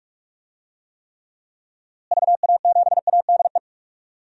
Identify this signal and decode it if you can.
Morse code, heard as a keyed tone, about 46 wpm, 700 Hz. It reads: VA7ADE
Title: DXCC-100 Call Signs - 6 Characters - Repeated 2x - 45wpm